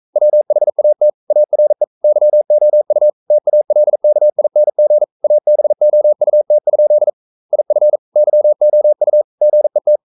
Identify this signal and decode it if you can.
Morse code, heard as a keyed tone, about 42 wpm, 600 Hz. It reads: WHAT ARE YOU TALKING ABOUT? IF YOU GET
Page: Romanen Never af Ken Follet på engelsk.